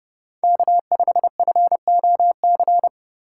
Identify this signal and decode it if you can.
Morse code, heard as a keyed tone, about 31 wpm, 700 Hz. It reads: K5FOC
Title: USA Call Signs - 5 Characters - 3x Character Spacing - 30wpm